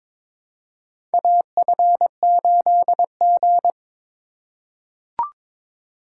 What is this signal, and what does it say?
Morse code, heard as a keyed tone, about 22 wpm, 700 Hz. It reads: AF8G
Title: USA Call Signs - 4 Characters - 3x Character Spacing - 22wpm